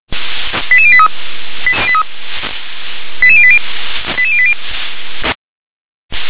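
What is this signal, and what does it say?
Five-tone selective call (ZVEI), heard as quick runs of five tones, tones of about 70 ms. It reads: 88083 88080 88080
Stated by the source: Received on an SDR as a ZVEI five-tone call, address 88603.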